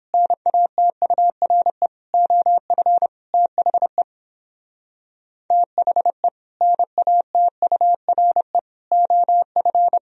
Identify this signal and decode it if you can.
Morse code, heard as a keyed tone, about 29 wpm, 700 Hz. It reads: NATURE OF THE THE NATURE OF